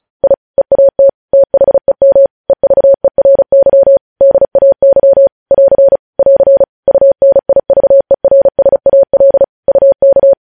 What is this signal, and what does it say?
Morse code, heard as a keyed tone, about 36 wpm, 550 Hz. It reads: I EAT THEM EVERY DAY <AR> <AR> UNIVERSAL UK
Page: news headlines in morse code at 35 wpm